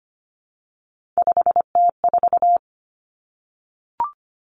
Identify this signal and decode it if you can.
Morse code, heard as a keyed tone, about 25 wpm, 700 Hz. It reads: 5T4
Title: Three Challenging Letter-Number Combinations 25wpm